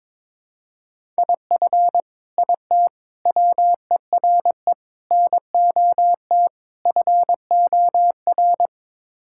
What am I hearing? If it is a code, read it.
Morse code, heard as a keyed tone, about 21 wpm, 700 Hz. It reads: IF IT WERE NOT FOR
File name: practice / 22wpm / Build A Phrase - 5-Word Phrases - 22wpm